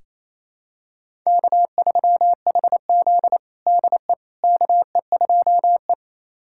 Morse code, heard as a keyed tone, about 28 wpm, 700 Hz. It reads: K3HZ DE KE2E
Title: USA Callsign DE Callsign - 4 Characters - 28wpm